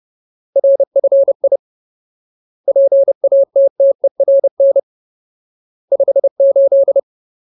Morse code, heard as a keyed tone, about 30 wpm, 550 Hz. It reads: RFI PATTERN 58
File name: Ultimate Word List / 30WPM Ultimate